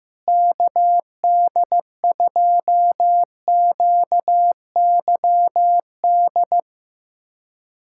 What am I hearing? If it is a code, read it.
Morse code, heard as a keyed tone, about 15 wpm, 700 Hz. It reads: KD2QYD
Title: USA Callsign DE Callsign - 6 Characters - 15wpm